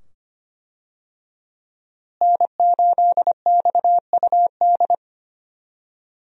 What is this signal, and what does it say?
Morse code, heard as a keyed tone, about 25 wpm, 700 Hz. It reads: N8XUD
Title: USA Call Signs - 5 Characters - Repeated 2x - 25wpm